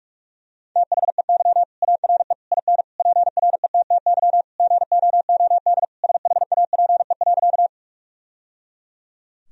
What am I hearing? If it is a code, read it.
Morse code, heard as a keyed tone, about 46 wpm, 700 Hz. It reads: THEY ARE IN PRETTY GOOD SHAPE.
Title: Sentences from Top 1000 Words - Review Part 3 - 45wpm